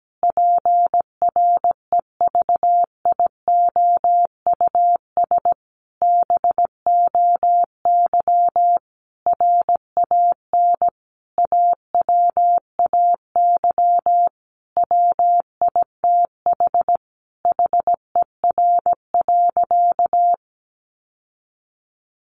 Morse code, heard as a keyed tone, about 17 wpm, 700 Hz. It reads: PREVIOUS BOY RAN AWAY WITH HER.
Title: Sentences from Top 1500 Words 17wpm